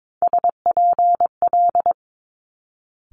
Morse code, heard as a keyed tone, about 22 wpm, 700 Hz. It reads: SPL
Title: Multi-letter Phonemes Quick Fire 22wpm